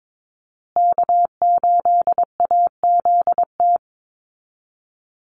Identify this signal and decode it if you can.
Morse code, heard as a keyed tone, about 22 wpm, 700 Hz. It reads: K8AZT
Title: Call Signs Repeated 2x - 22wpm